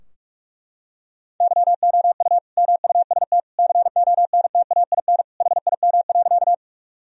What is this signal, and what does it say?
Morse code, heard as a keyed tone, about 41 wpm, 700 Hz. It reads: YOU MUST CONTAIN HIM.
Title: Sentences from Top 1000 Words - Review Part 2 - 45wpm